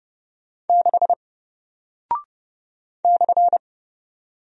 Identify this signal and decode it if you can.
Morse code, heard as a keyed tone, about 30 wpm, 700 Hz. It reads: 6 /
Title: Single Character (Letter:Number:Punctuation:Prosign) Alt 30wpm